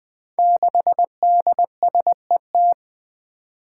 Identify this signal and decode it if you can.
Morse code, heard as a keyed tone, about 21 wpm, 700 Hz. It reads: 6DSET
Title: Five Challenging Letter-Number Combinations 20wpm